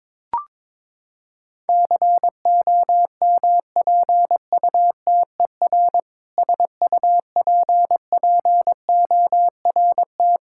Morse code, heard as a keyed tone, about 21 wpm, 700 Hz. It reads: COMPUTER SUPPORT
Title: Practice QSO - Exchange 3 - More Info - 22wpm